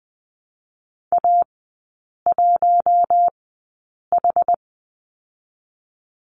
Morse code, heard as a keyed tone, about 20 wpm, spaced as for 8 wpm, 690 Hz. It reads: A1H
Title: 081 Lesson 14e - Number 2 - Three Characters 20wpm